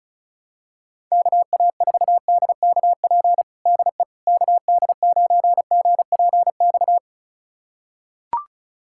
Morse code, heard as a keyed tone, about 36 wpm, 700 Hz. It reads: KA4DKP DE KD9GPX
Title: USA Callsign DE Callsign 35wpm